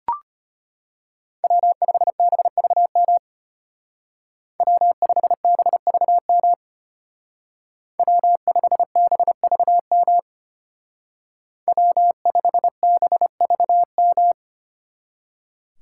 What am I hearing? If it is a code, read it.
Morse code, heard as a keyed tone, about 33 wpm, 700 Hz. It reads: W5BVM W5BVM W5BVM W5BVM